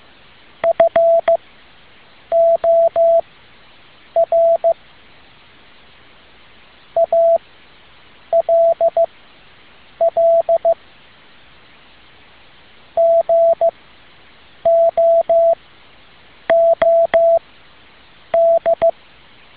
Morse code, heard as a keyed tone, about 15 wpm, spaced as for 7 wpm, 670 Hz. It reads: FOR ALL GOOD
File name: tin-upgrade / QSO 2013 02 12